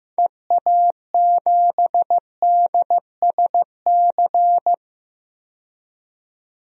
Morse code, heard as a keyed tone, about 15 wpm, 690 Hz. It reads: EA7DSC
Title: DXCC-100 Call Signs - 6 Characters - 15wpm